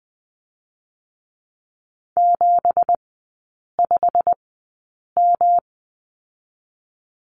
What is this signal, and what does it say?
Morse code, heard as a keyed tone, about 20 wpm, spaced as for 8 wpm, 690 Hz. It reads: 75M
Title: 172 Lesson 28e - Number 9 - Three Characters 20wpm